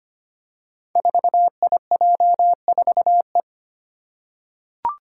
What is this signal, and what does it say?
Morse code, heard as a keyed tone, about 25 wpm, 700 Hz. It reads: 4IJ4E
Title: Five Challenging Letter-Number Combinations 25wpm